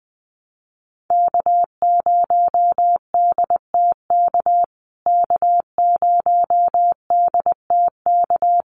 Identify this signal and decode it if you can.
Morse code, heard as a keyed tone, about 20 wpm, 700 Hz. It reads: K0DTK K0DTK
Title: USA Call Signs Repeated 3x 20wpm